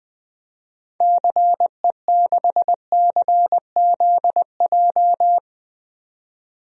Morse code, heard as a keyed tone, about 21 wpm, 700 Hz. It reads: CE6CZJ
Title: DXCC-100 Call Signs - 6 Characters - 20wpm